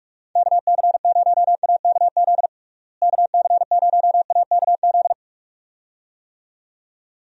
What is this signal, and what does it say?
Morse code, heard as a keyed tone, about 45 wpm, 700 Hz. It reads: KC0AKZ KC0AKZ
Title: USA Call Signs - 6 Characters - Repeated 2x - 45wpm